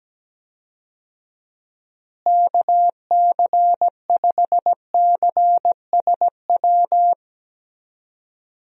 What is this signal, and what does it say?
Morse code, heard as a keyed tone, about 17 wpm, 700 Hz. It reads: KC5CSW
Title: USA Call Signs - 6 Characters - Repeated 3x - 17wpm